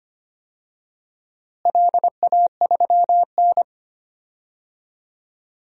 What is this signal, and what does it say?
Morse code, heard as a keyed tone, about 25 wpm, 700 Hz. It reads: LA3N
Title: DXCC-100 Call Signs - 4 Characters - Repeated 2x - 25wpm